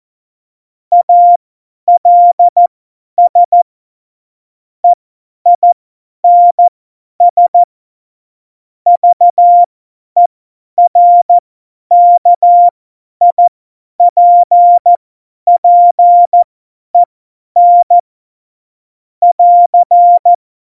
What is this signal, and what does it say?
Morse code, heard as a keyed tone, about 14 wpm, spaced as for 10 wpm, 700 Hz. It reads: ALS EINS VERKIPPEN <AR>
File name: Morsekurs / Witze / Witz 01. Oktober 2019, 50 F70